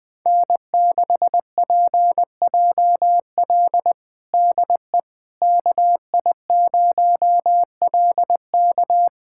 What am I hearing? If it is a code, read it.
Morse code, heard as a keyed tone, about 21 wpm, 700 Hz. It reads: N6PJL DE KI0LK
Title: USA Callsign DE Callsign - 5 Characters - 20wpm